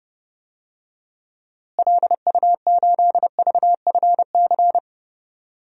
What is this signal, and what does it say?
Morse code, heard as a keyed tone, about 30 wpm, 700 Hz. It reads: LU8VFC
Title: DXCC-100 Call Signs - 6 Characters - Repeated 3x - 30wpm